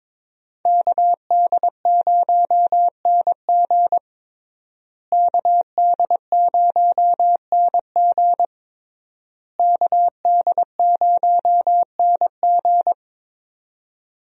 Morse code, heard as a keyed tone, about 22 wpm, 700 Hz. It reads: KD0NG KD0NG KD0NG
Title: USA Call Signs - 5 Characters - Repeated 3x - 22wpm